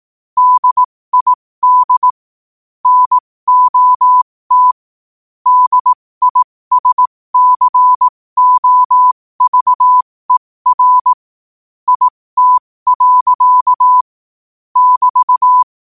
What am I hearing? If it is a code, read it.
Morse code, heard as a keyed tone, about 18 wpm, spaced as for 15 wpm, 1000 Hz. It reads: DID NOT DISCOVER IT. <BT>
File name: qotdfiles / 15wpm